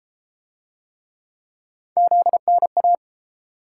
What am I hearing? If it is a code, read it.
Morse code, heard as a keyed tone, about 30 wpm, 700 Hz. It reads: ZNA